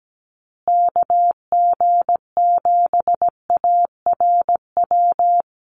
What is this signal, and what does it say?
Morse code, heard as a keyed tone, about 17 wpm, 700 Hz. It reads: KG7ARW
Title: USA Call Signs Repeated 3x 17wpm